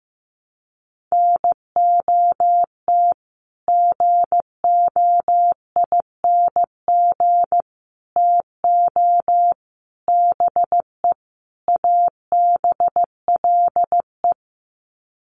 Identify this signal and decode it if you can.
Morse code, heard as a keyed tone, about 15 wpm, 700 Hz. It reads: NOT GOING TO BE ABLE
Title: Sets of 5 Words 15wpm